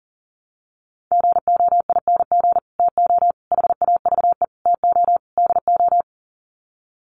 Morse code, heard as a keyed tone, about 40 wpm, 700 Hz. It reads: GOING TO HAVE TO DO
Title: Sets of 5 Words Encore 40wpm